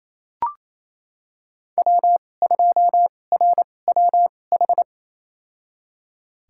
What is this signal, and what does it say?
Morse code, heard as a keyed tone, about 28 wpm, spaced as for 20 wpm, 700 Hz. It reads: W2RWH
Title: USA Call Signs - 5 Characters - 2x Character Spacing - 28wpm